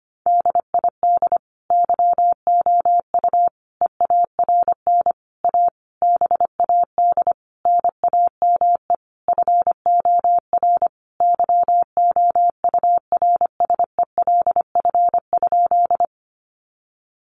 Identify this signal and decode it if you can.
Morse code, heard as a keyed tone, about 25 wpm, 700 Hz. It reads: DID YOU EARN A BAD NAME FOR YOURSELF?
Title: Sentences from Top 1500 Words 25wpm